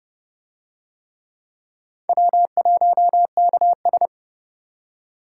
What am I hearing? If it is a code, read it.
Morse code, heard as a keyed tone, about 29 wpm, 700 Hz. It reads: W1KS